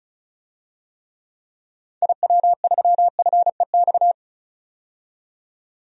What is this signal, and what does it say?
Morse code, heard as a keyed tone, about 36 wpm, 700 Hz. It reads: IW3FEX
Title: DXCC-100 Call Signs - 6 Characters - 35wpm